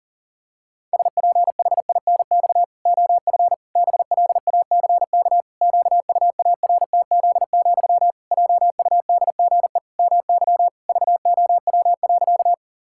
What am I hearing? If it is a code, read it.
Morse code, heard as a keyed tone, about 40 wpm, 700 Hz. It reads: SPHINX OF BLACK QUARTZ, JUDGE MY VOW.
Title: Pangram Challenge v2 - 40wpm